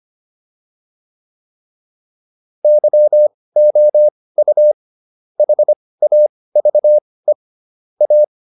Morse code, heard as a keyed tone, about 25 wpm, spaced as for 18 wpm, 600 Hz. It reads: YOU HAVE A
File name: WD204 18